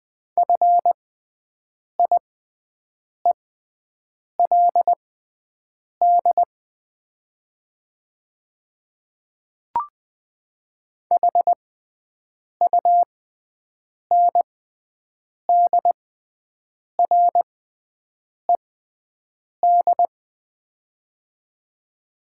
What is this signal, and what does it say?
Morse code, heard as a keyed tone, about 20 wpm, spaced as for 7 wpm, 700 Hz. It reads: FIELD HUNDRED
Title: 6x Character Spacing - Top 500 Words - 20wpm